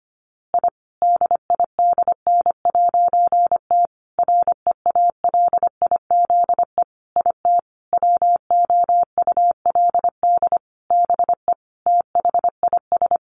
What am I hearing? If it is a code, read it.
Morse code, heard as a keyed tone, about 25 wpm, 700 Hz. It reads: I DIDN'T REALIZE IT WOULD BE THIS
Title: A Sound Of Thunder 25wpm